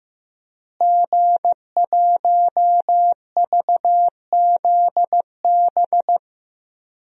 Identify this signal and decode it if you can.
Morse code, heard as a keyed tone, about 15 wpm, 700 Hz. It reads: G1VZB
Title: DXCC-100 Call Signs - 5 Characters - Repeated 3x - 15wpm